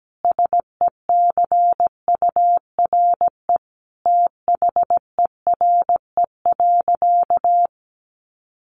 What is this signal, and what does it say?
Morse code, heard as a keyed tone, about 17 wpm, 700 Hz. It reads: SECURE THERE.
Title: Sentences from Top 1200 Words 17wpm